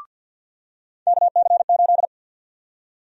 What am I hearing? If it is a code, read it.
Morse code, heard as a keyed tone, about 52 wpm, 700 Hz. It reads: KC8
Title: USA Call Sign Prefixes - 50wpm